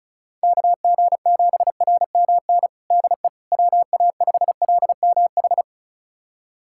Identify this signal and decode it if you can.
Morse code, heard as a keyed tone, about 36 wpm, 700 Hz. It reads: KG7RMN DE WA5LMH
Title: USA Callsign DE Callsign - 6 Characters - 35wpm